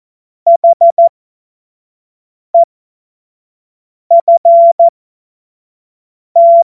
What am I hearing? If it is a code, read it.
Morse code, heard as a keyed tone, about 14 wpm, spaced as for 5 wpm, 680 Hz. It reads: HEFT
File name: Morsekurs / Witze / Witz 19. April 2022 25 F70